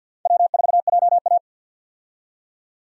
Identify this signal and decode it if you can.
Morse code, heard as a keyed tone, about 53 wpm, 700 Hz. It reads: W4JA
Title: USA Call Signs - 4 Characters - 50wpm